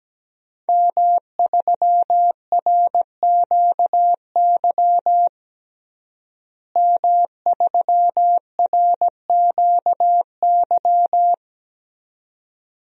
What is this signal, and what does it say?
Morse code, heard as a keyed tone, about 17 wpm, 700 Hz. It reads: M3RQY M3RQY
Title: DXCC-100 Call Signs - 5 Characters - Repeated 2x - 17wpm